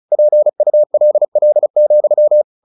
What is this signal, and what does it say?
Morse code, heard as a keyed tone, about 34 wpm, 600 Hz. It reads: PULL,
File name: the old man and the sea / the old man and the sea 1 35wpm